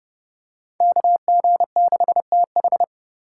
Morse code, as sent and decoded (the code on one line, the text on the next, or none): -.- --. -.... - ....
KG6TH